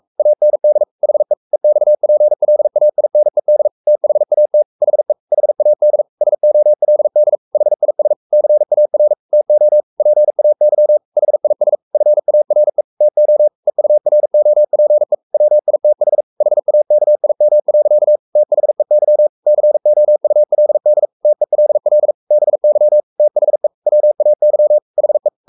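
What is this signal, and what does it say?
Morse code, heard as a keyed tone, about 44 wpm, 600 Hz. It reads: AND HE EXPLAINED THAT HE HAD SOLD HIS CAR TO PAY HIS FARE TO EUROPE WITH HAKIM. THEY COULD TELL BY THE WAY HE
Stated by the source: Morse code at 43 wpm sending Ken Follett's novel Never in English, from CW med Gnister.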